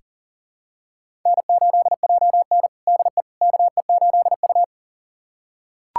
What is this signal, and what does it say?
Morse code, heard as a keyed tone, about 40 wpm, 700 Hz. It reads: N8JN DE KE8U
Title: USA Callsign DE Callsign - 4 Characters - 40wpm